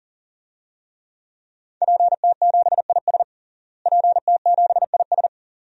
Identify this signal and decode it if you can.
Morse code, heard as a keyed tone, about 40 wpm, 700 Hz. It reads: PT7IS PT7IS
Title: DXCC-100 Call Signs - 5 Characters - Repeated 2x - 40wpm